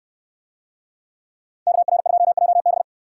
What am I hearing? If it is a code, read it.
Morse code, heard as a keyed tone, about 67 wpm, 700 Hz. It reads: KR2JB